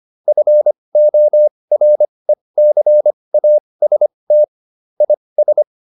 Morse code, heard as a keyed tone, about 25 wpm, spaced as for 20 wpm, 600 Hz. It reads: FORECAST IS